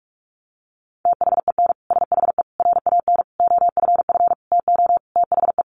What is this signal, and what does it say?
Morse code, heard as a keyed tone, about 44 wpm, 700 Hz. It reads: THEN SHE RAN OFF TO THE
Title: Sentences from Top 1400 Words 45wpm